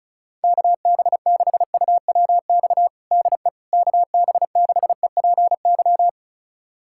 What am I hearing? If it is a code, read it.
Morse code, heard as a keyed tone, about 34 wpm, 690 Hz. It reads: KB6UWX DE KB6EPY
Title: USA Callsign DE Callsign 35wpm